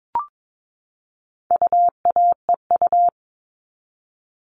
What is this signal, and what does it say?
Morse code, heard as a keyed tone, about 22 wpm, 700 Hz. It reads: UAEU
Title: All Cut Numbers - 4-Digit 22wpm